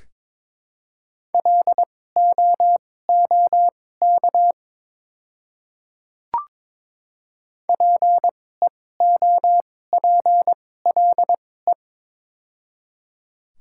Morse code, heard as a keyed tone, about 23 wpm, spaced as for 16 wpm, 700 Hz. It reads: LOOK PEOPLE
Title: 2x Character Spacing - Top 100 Words - 22wpm